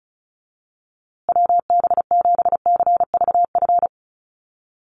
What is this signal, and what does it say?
Morse code, heard as a keyed tone, about 34 wpm, 700 Hz. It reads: WB7CVF
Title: USA Call Signs Repeated 2x 35wpm